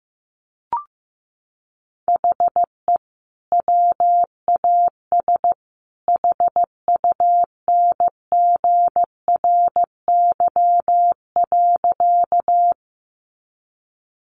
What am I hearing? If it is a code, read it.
Morse code, heard as a keyed tone, about 15 wpm, 700 Hz. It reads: HE WAS HUNGRY.
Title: Sentences from Top 1400 Words 15wpm